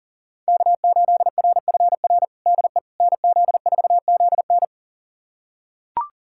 Morse code, heard as a keyed tone, about 40 wpm, 700 Hz. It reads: K8RFR DE NZ4ZN
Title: USA Callsign DE Callsign - 5 Characters - 40wpm